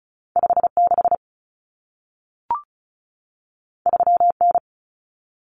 Morse code, heard as a keyed tone, about 36 wpm, 700 Hz. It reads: 56 3N
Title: Common Cut Numbers - 2-Digit 35wpm